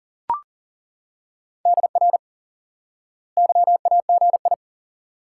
Morse code, heard as a keyed tone, about 40 wpm, 700 Hz. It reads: DR YAGI
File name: IQE Course 3 - Lesson 89 - YAGI - Two Elements - 40wpm